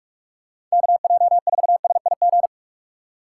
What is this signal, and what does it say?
Morse code, heard as a keyed tone, about 44 wpm, 700 Hz. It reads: KJ4SIG
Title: USA Call Signs - 6 Characters - 4x Character Spacing - 45wpm